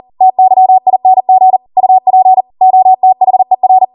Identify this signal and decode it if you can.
Morse code, heard as a keyed tone, about 40 wpm, 750 Hz. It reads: TYING UP OTHER